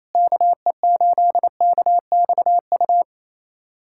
Morse code, heard as a keyed tone, about 28 wpm, 700 Hz. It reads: KE8KXU
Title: USA Callsign DE Callsign 28wpm